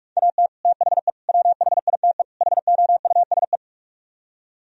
Morse code, heard as a keyed tone, about 49 wpm, 700 Hz. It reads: AT THE WHITE HOUSE